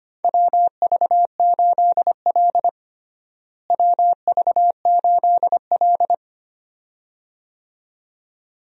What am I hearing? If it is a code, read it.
Morse code, heard as a keyed tone, about 25 wpm, 700 Hz. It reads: WV8L WV8L
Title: USA Call Signs - 4 Characters - Repeated 2x - 25wpm